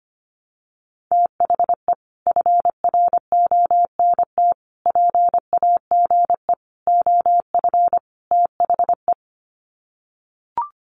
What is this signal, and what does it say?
Morse code, heard as a keyed tone, about 25 wpm, 700 Hz. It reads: THE FRONT PAGE OF THE
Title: Sets of 5 Words Encore 25wpm